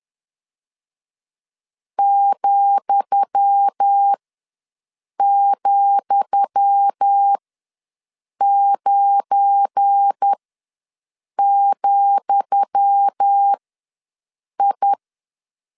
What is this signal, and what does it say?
Morse code, heard as a keyed tone, about 10 wpm, spaced as for 6 wpm, 800 Hz. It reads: ,,9,I
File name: LEZIONI / 10 SIGNS / 10D COMA ws 10wpm 4314